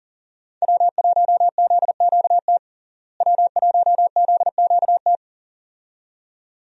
Morse code, heard as a keyed tone, about 38 wpm, 700 Hz. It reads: W1ZQT W1ZQT
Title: USA Call Signs - 5 Characters - Repeated 2x - 40wpm